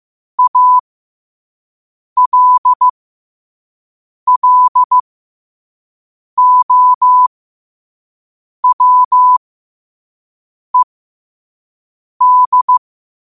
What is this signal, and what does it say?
Morse code, heard as a keyed tone, about 15 wpm, spaced as for 5 wpm, 1000 Hz. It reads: ALLOWED
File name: qotdfiles / 5wpm